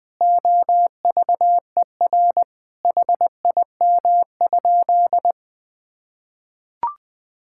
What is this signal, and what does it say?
Morse code, heard as a keyed tone, about 20 wpm, 700 Hz. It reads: OVER HIM?
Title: Sentences from Top 1600 Words 20wpm